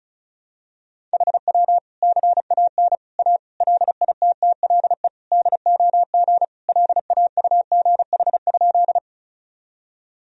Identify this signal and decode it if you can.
Morse code, heard as a keyed tone, about 34 wpm, 700 Hz. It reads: HW CAN A LITTLE DOG LAUGH?
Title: Sentences from Top 500 Words and Abbreviations 35wpm